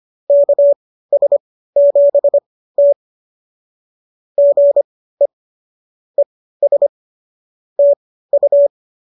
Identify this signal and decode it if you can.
Morse code, heard as a keyed tone, about 25 wpm, spaced as for 15 wpm, 590 Hz. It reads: KS7T GE ES TU